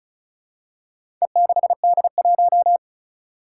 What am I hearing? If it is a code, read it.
Morse code, heard as a keyed tone, about 36 wpm, 700 Hz. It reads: E6D1
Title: Four Challenging Letter-Number Combinations 35wpm